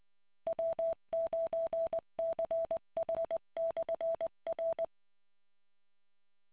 Morse code, heard as a keyed tone, about 20 wpm, 660 Hz. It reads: W9CH/R